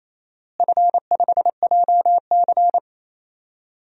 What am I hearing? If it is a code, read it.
Morse code, heard as a keyed tone, about 28 wpm, 700 Hz. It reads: F5JC
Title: DXCC-100 Call Signs - 4 Characters - Repeated 3x - 28wpm